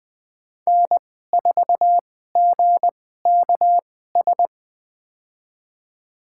Morse code, heard as a keyed tone, about 20 wpm, spaced as for 14 wpm, 700 Hz. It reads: N4GKS
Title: USA Call Signs - 5 Characters - 2x Character Spacing - 20wpm